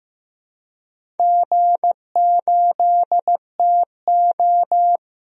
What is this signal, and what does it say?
Morse code, heard as a keyed tone, about 15 wpm, 700 Hz. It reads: G8TO
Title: DXCC-100 Call Signs - 4 Characters - Repeated 3x - 15wpm